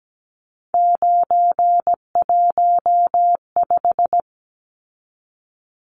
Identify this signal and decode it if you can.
Morse code, heard as a keyed tone, about 17 wpm, 700 Hz. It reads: 915
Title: Numbers - 3-Digit 17wpm